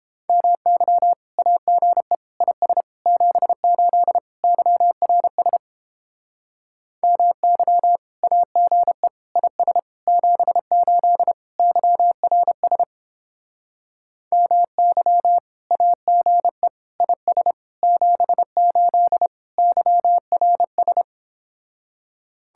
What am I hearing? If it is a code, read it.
Morse code, heard as a keyed tone, about 29 wpm, 700 Hz. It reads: MY AGE IS 78 YRS MY AGE IS 78 YRS MY AGE IS 78 YRS